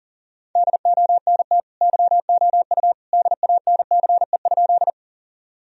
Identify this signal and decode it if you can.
Morse code, heard as a keyed tone, about 40 wpm, 700 Hz. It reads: DONT YOU DANCE?
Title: Sentences from Top 1000 Words - Review Part 4 - 40wpm